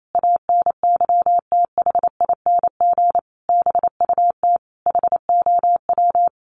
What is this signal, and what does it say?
Morse code, heard as a keyed tone, about 28 wpm, 700 Hz. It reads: ANYTHING BUT HOW
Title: Sentences from Top 1000 Words 28wpm